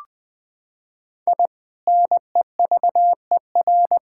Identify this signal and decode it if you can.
Morse code, heard as a keyed tone, about 20 wpm, 700 Hz. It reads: I NEVER
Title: Sentences from Top 1000 Words - Review Part 6 - 20wpm